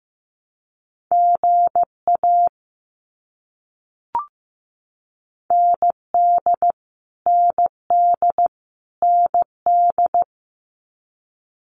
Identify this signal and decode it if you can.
Morse code, heard as a keyed tone, about 15 wpm, 700 Hz. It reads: GA ND ND ND
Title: US States Repeated 3x 15wpm